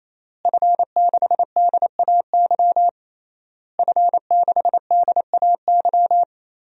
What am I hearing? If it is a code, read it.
Morse code, heard as a keyed tone, about 28 wpm, 700 Hz. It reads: F6DAY F6DAY
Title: DXCC-100 Call Signs - 5 Characters - Repeated 3x - 28wpm